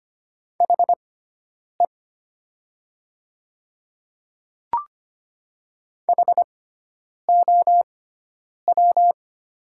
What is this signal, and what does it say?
Morse code, heard as a keyed tone, about 25 wpm, spaced as for 9 wpm, 700 Hz. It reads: HE HOW
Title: 6x Character Spacing - Top 100 Words - 25wpm